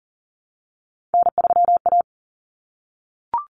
Morse code, heard as a keyed tone, about 40 wpm, 700 Hz. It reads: N3A
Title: Common Cut Numbers - 3-Digit 40wpm